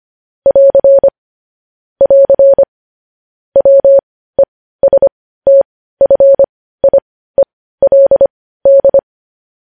Morse code, heard as a keyed tone, about 25 wpm, spaced as for 15 wpm, 550 Hz. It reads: <AR> <AR> WESTFIELD